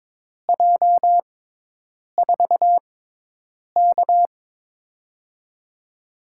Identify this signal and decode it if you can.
Morse code, heard as a keyed tone, about 22 wpm, spaced as for 8 wpm, 700 Hz. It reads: J4K
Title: USA Call Signs - 4 Characters - 6x Character Spacing - 22wpm